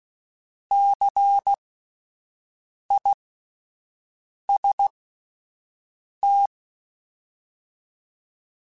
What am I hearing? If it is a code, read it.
Morse code, heard as a keyed tone, about 16 wpm, spaced as for 5 wpm, 800 Hz. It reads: CIST
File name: Alphabetical / Alphabetical mp3s / Cs @5